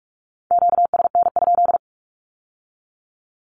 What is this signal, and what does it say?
Morse code, heard as a keyed tone, about 46 wpm, 700 Hz. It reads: QSN?
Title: Q-Codes with Spoken Meaning - 45wpm